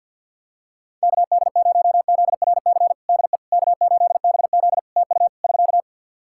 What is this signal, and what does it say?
Morse code, heard as a keyed tone, about 51 wpm, 700 Hz. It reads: KD0ZRC DE K8BZ TU <SK>
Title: Practice QSO - Exchange 4 - Ending - 50wpm